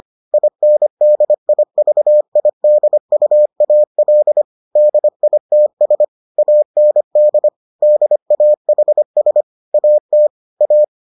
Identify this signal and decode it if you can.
Morse code, heard as a keyed tone, about 25 wpm, 600 Hz. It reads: INDIVIDUAL DITS AND DAHS AT A